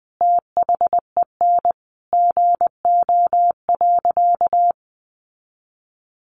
Morse code, heard as a keyed tone, about 20 wpm, 700 Hz. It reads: THEN GO.
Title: Sentences from Top 1300 Words 20wpm